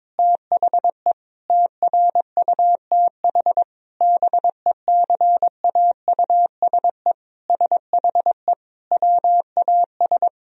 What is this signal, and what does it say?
Morse code, heard as a keyed tone, about 22 wpm, 700 Hz. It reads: THE TRUTH BECAUSE SHE WAS
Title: Sentences from Top 1000 Words - Review Part 3 - 22wpm